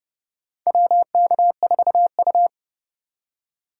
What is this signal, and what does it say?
Morse code, heard as a keyed tone, about 30 wpm, 700 Hz. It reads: WK4U
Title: USA Call Signs - 4 Characters - 4x Character Spacing - 30wpm